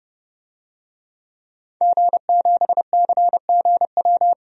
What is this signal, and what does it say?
Morse code, heard as a keyed tone, about 32 wpm, 700 Hz. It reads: G7CGW
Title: DXCC-100 Call Signs - 5 Characters - 30wpm